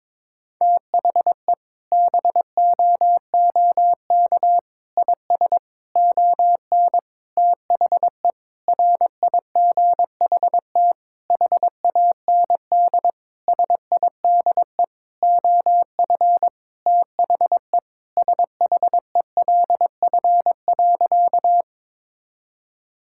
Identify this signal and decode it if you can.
Morse code, heard as a keyed tone, about 22 wpm, 700 Hz. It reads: THE BOOK IS ON THE RIGHT HAND SIDE OF THE SHELF.
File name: Sentences with 7-Word Phrases - 22wpm